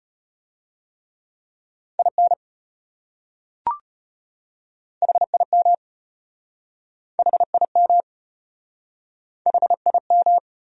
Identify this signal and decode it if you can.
Morse code, heard as a keyed tone, about 33 wpm, 700 Hz. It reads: IN HIM HIM HIM